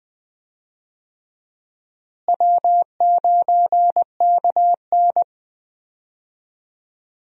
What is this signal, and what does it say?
Morse code, heard as a keyed tone, about 20 wpm, 700 Hz. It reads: W9KN
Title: USA Call Signs - 4 Characters - Repeated 2x - 20wpm